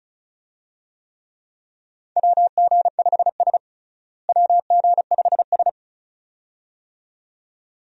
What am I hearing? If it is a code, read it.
Morse code, heard as a keyed tone, about 34 wpm, 700 Hz. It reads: WG5S WG5S
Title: USA Call Signs - 4 Characters - Repeated 2x - 35wpm